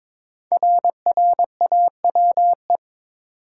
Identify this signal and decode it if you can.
Morse code, heard as a keyed tone, about 22 wpm, 700 Hz. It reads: RRAWE
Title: Code Groups - 5 Letters - 22wpm